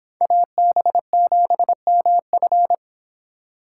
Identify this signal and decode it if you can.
Morse code, heard as a keyed tone, about 26 wpm, 700 Hz. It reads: AB7MF